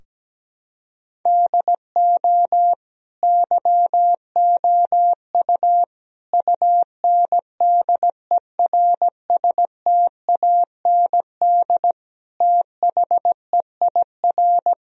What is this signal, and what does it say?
Morse code, heard as a keyed tone, about 17 wpm, 700 Hz. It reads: DO YOU UNDERSTAND THEIR
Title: Sentences from Top 1000 Words - Review Part 3 - 17wpm